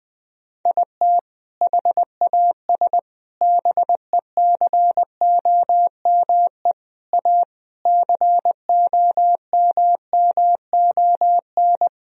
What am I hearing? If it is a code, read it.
Morse code, heard as a keyed tone, about 19 wpm, 700 Hz. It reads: IT HAS BECOME A COMMON
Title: Sentences from Top 1000 Words - Review Part 3 - 20wpm